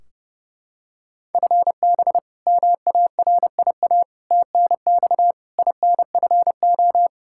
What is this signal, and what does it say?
Morse code, heard as a keyed tone, about 30 wpm, 700 Hz. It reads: FB MARIA TNX INFO
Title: K1USN SST QSO Practice 30wpm